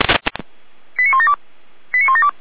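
Five-tone selective call (ZVEI), heard as quick runs of five tones, tones of about 70 ms. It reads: 89172 89172
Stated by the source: ZVEI five-tone call 89500 received with an SDR, archived by Feuerwehr Orlamünde.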